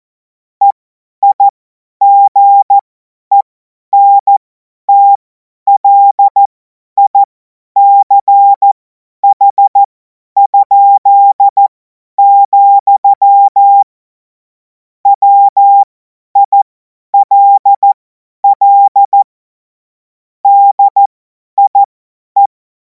Morse code, heard as a keyed tone, about 14 wpm, spaced as for 10 wpm, 800 Hz. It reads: EIGENTLICH?, WILL DIE
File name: Morsekurs / Witze / Witz 10. März 2026 50 F70